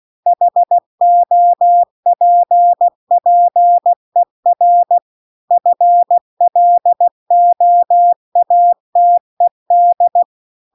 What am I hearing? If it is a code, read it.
Morse code, heard as a keyed tone, about 16 wpm, 700 Hz. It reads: HOPPER FLOATED